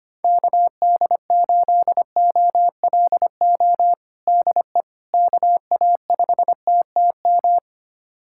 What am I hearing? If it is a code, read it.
Morse code, heard as a keyed tone, about 25 wpm, 700 Hz. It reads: KD8OLO DE KA5TTM
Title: USA Callsign DE Callsign 25wpm